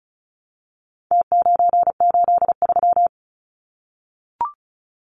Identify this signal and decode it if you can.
Morse code, heard as a keyed tone, about 34 wpm, 700 Hz. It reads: T983
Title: Numbers - Contest Style - 4-Digit 35wpm